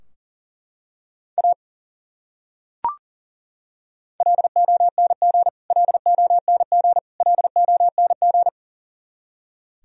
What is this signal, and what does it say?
Morse code, heard as a keyed tone, about 40 wpm, 700 Hz. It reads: A LONG LONG LONG
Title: Top 100 Words Repeated 40wpm